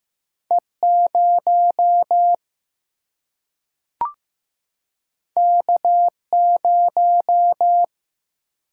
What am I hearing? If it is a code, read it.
Morse code, heard as a keyed tone, about 15 wpm, 700 Hz. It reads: E 0 K 0
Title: Single-Letter Single-Number 15wpm